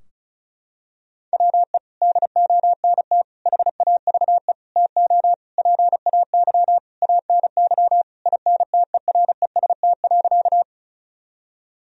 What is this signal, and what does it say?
Morse code, heard as a keyed tone, about 34 wpm, 700 Hz. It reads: WE DONT HAVE TO PAY ANY INTEREST.
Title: Sentences from Top 1000 Words - Review Part 2 - 35wpm